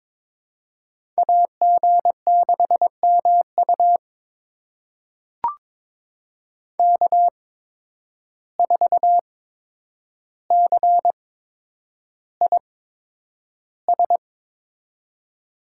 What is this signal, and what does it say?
Morse code, heard as a keyed tone, about 22 wpm, spaced as for 9 wpm, 700 Hz. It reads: AG6MU K4CIS
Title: USA Call Signs - 5 Characters - 8x Character Spacing - 22wpm